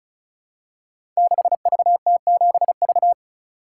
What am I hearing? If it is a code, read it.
Morse code, heard as a keyed tone, about 36 wpm, 700 Hz. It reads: 6VT7V
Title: Code Groups - 5 Letter:Numbers - 35wpm